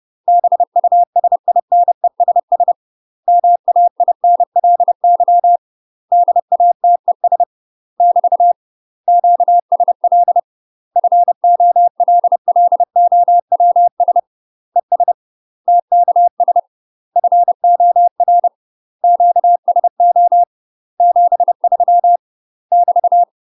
Morse code, heard as a keyed tone, about 30 wpm, 700 Hz. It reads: BUSINESS MAINLY DATES <BT> QSL FOLLOWS ES TKS FOR QSO 73 <BT>
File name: UK MASTER QSOs TRAINING / 30WPM / UK MASTER QSOs 0023